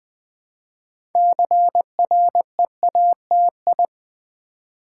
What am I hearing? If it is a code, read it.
Morse code, heard as a keyed tone, about 20 wpm, 700 Hz. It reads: CREATI
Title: Build A Word - 2 Letters At A Time - 8-Letter Words - 20wpm